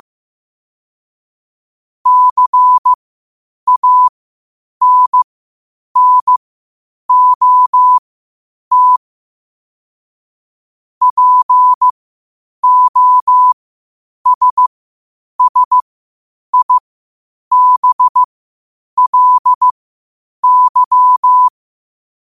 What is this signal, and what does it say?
Morse code, heard as a keyed tone, about 15 wpm, spaced as for 8 wpm, 1000 Hz. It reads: CANNOT POSSIBLY
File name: qotdfiles / 8wpm